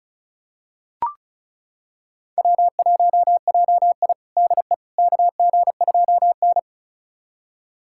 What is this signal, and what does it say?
Morse code, heard as a keyed tone, about 36 wpm, 700 Hz. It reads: W1JI DE KG2N
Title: USA Callsign DE Callsign - 4 Characters - 35wpm